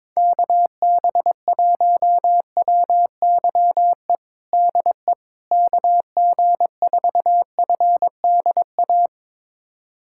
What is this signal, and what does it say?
Morse code, heard as a keyed tone, about 21 wpm, 700 Hz. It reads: KB1WYE DE KG4FDA
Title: USA Callsign DE Callsign - 6 Characters - 22wpm